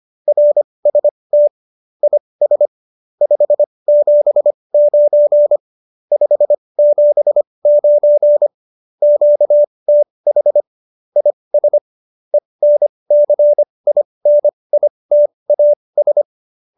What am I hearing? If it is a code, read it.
Morse code, heard as a keyed tone, about 25 wpm, spaced as for 20 wpm, 590 Hz. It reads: RST IS 579 579 QTH IS ENCINITAS
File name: QSO307 20